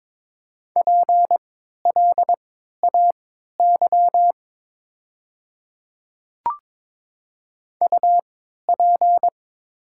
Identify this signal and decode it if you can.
Morse code, heard as a keyed tone, about 22 wpm, spaced as for 13 wpm, 700 Hz. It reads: PLAY UP
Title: 3x Character Spacing - Top 200 Words - 22wpm